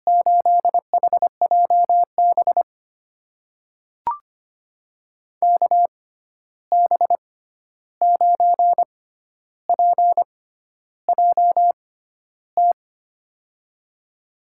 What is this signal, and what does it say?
Morse code, heard as a keyed tone, about 25 wpm, spaced as for 11 wpm, 700 Hz. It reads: 8HJB KB9PJT
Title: USA Call Signs - 6 Characters - 6x Character Spacing - 25wpm